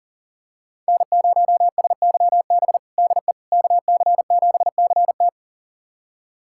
Morse code, heard as a keyed tone, about 40 wpm, 700 Hz. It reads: N0SYB DE KC7CT
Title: USA Callsign DE Callsign - 5 Characters - 40wpm